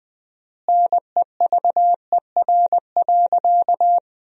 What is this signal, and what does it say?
Morse code, heard as a keyed tone, about 20 wpm, 700 Hz. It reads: NEVER.
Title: 4x Word-Spacing - Sentences from Top 300 Words - 20wpm